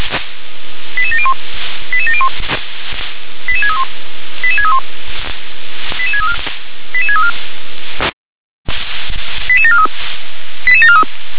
Five-tone selective call (ZVEI), heard as quick runs of five tones, tones of about 70 ms. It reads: 88701 88701 88631 88631 88634 88634 88743 88743